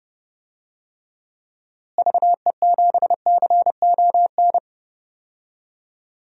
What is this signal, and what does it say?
Morse code, heard as a keyed tone, about 30 wpm, 700 Hz. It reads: VE7CON
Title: DXCC-100 Call Signs - 6 Characters - Repeated 2x - 30wpm